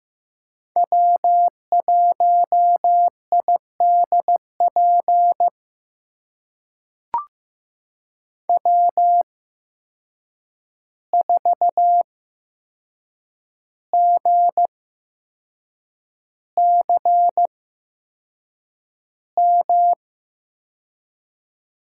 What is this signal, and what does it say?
Morse code, heard as a keyed tone, about 15 wpm, spaced as for 6 wpm, 700 Hz. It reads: W1IDP W4GCM
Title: USA Call Signs - 5 Characters - 8x Character Spacing - 15wpm